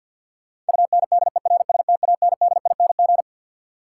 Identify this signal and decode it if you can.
Morse code, heard as a keyed tone, about 53 wpm, 700 Hz. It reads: UNDERSTANDING
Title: Sentences from Top 1000 Words - Review Part 3 - New Word Drill - 50wpm